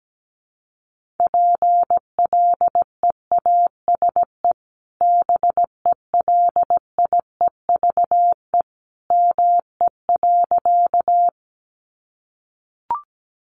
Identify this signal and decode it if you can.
Morse code, heard as a keyed tone, about 17 wpm, 700 Hz. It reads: PLEASE BELIEVE ME.
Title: Sentences from Top 1500 Words 17wpm